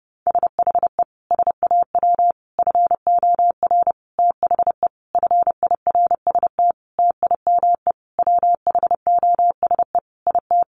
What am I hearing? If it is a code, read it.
Morse code, heard as a keyed tone, about 30 wpm, 700 Hz. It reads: SHE SAW FOR THE FIRST TIME WHOSE IT
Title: Sentences from Top 1200 Words 30wpm